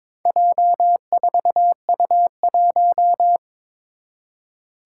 Morse code, heard as a keyed tone, about 22 wpm, 700 Hz. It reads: J4U1
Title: Four Challenging Letter-Number Combinations 22wpm